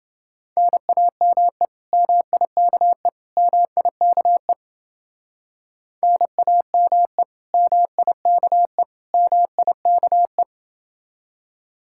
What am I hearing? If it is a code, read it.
Morse code, heard as a keyed tone, about 30 wpm, 700 Hz. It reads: NAME MIKE MIKE NAME MIKE MIKE